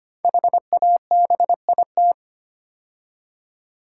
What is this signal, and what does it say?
Morse code, heard as a keyed tone, about 25 wpm, 700 Hz. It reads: HABIT
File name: 5 Letter Words - First 3 Letters Given - 25wpm